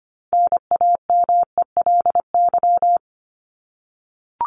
Morse code, heard as a keyed tone, about 25 wpm, 700 Hz. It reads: NAMELY
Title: LY Suffix - Top 100 Words with Base Words 25wpm